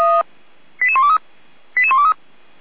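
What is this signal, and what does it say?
Five-tone selective call (ZVEI), heard as quick runs of five tones, tones of about 70 ms. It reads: 88123 88123
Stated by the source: ZVEI five-tone call received by SDR, fire brigade Orlamünde (feuerwehr-orlamuende).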